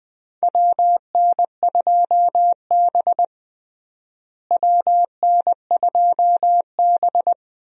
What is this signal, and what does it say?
Morse code, heard as a keyed tone, about 20 wpm, 700 Hz. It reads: WN2B WN2B
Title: USA Call Signs - 4 Characters - Repeated 3x - 20wpm